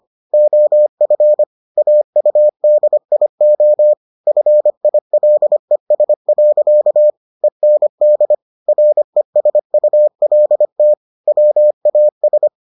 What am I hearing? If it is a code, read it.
Morse code, heard as a keyed tone, about 25 wpm, 600 Hz. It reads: OF AUDIO FILES. END RESULT WAS